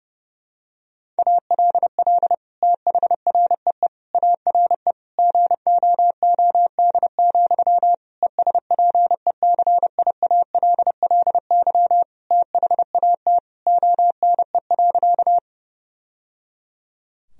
Morse code, heard as a keyed tone, about 31 wpm, 700 Hz. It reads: ALL THREE ARE GOOD, ESPECIALLY THAT ONE.
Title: Sentences from Top 1000 Words - Review Part 4 - 30wpm